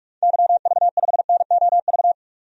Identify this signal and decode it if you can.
Morse code, heard as a keyed tone, about 45 wpm, 700 Hz. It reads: YV5NOV
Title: DXCC-100 Call Signs - 6 Characters - 45wpm